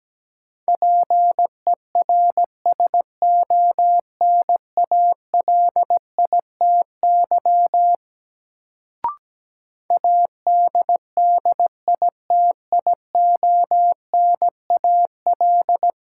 Morse code, heard as a keyed tone, about 17 wpm, 700 Hz. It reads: PERSONALITY ADDITIONAL
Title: Sentences from Top 1400 Words - New Word Drill Alt - 17wpm